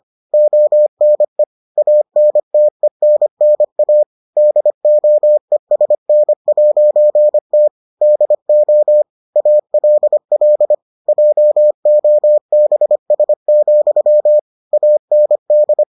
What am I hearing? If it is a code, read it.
Morse code, heard as a keyed tone, about 25 wpm, 600 Hz. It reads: ONE ANTENNA DOESN'T DO ALL JOBS, AND